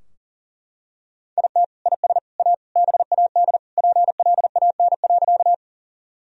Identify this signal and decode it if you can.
Morse code, heard as a keyed tone, about 40 wpm, 700 Hz. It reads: IT IS A BAD PLAN.
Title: Sentences from Top 1000 Words - Review Part 2 - 40wpm